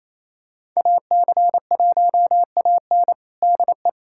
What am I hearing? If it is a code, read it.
Morse code, heard as a keyed tone, about 27 wpm, 700 Hz. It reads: AC1AN DE
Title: USA Callsign DE Callsign - 5 Characters - 28wpm